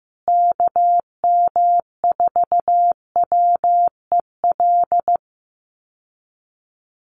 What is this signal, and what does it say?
Morse code, heard as a keyed tone, about 15 wpm, 700 Hz. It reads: KM4WEL
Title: USA Call Signs Repeated 2x 15wpm